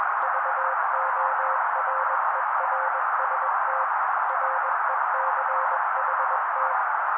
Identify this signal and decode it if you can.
Morse code, heard as a keyed tone, about 21 wpm, 530 Hz. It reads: VORERST RECHT